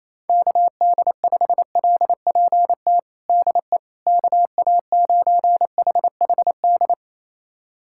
Morse code, heard as a keyed tone, about 28 wpm, 700 Hz. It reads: KD5LPT DE KA9HHD
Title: USA Callsign DE Callsign 28wpm